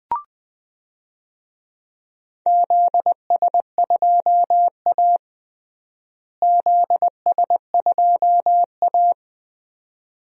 Morse code, heard as a keyed tone, about 21 wpm, 700 Hz. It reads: ZS2A ZS2A
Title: DXCC-100 Call Signs - 4 Characters - Repeated 2x - 20wpm